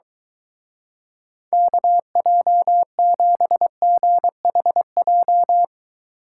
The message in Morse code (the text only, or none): KJ7GHJ